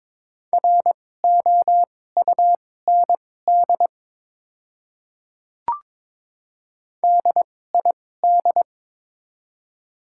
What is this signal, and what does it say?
Morse code, heard as a keyed tone, about 22 wpm, spaced as for 16 wpm, 700 Hz. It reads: ROUND DID
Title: 2x Character Spacing - Top 300 Words - 22wpm